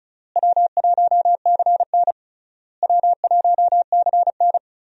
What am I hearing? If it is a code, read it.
Morse code, heard as a keyed tone, about 36 wpm, 700 Hz. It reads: W1CN W1CN
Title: USA Call Signs - 4 Characters - Repeated 3x - 35wpm